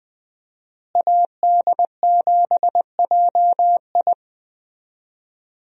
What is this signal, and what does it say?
Morse code, heard as a keyed tone, about 21 wpm, 700 Hz. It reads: AD7JI
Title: USA Call Signs - 5 Characters - Repeated 2x - 20wpm